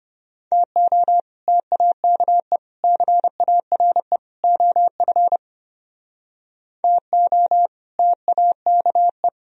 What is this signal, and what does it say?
Morse code, heard as a keyed tone, about 32 wpm, 700 Hz. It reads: TO TAKE CARE OF TO TAKE